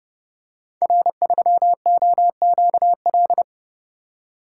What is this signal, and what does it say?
Morse code, heard as a keyed tone, about 30 wpm, 700 Hz. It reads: R3OQL
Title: Code Groups - 5 Letter:Numbers - 30wpm